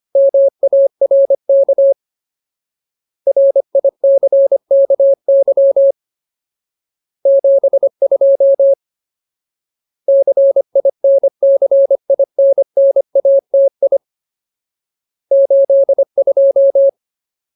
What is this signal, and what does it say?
Morse code, heard as a keyed tone, about 25 wpm, 550 Hz. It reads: MARK RICKY 72 CINCINNATI 82
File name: Ultimate Word List / 25WPM Ultimate